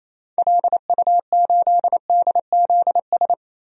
Morse code, heard as a keyed tone, about 28 wpm, 700 Hz. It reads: LU8DZS
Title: DXCC-100 Call Signs - 6 Characters - 28wpm